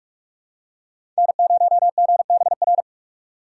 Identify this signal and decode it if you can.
Morse code, heard as a keyed tone, about 42 wpm, 700 Hz. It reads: N0GBR
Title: USA Call Signs - 5 Characters - 2x Character Spacing - 45wpm